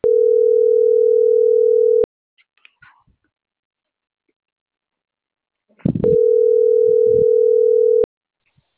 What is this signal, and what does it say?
Ringback tone (North America), 440 + 480 Hz.